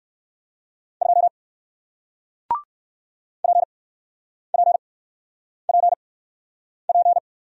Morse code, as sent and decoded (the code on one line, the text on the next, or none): ..--- / .--. .--. .--. .--.
2 PPPP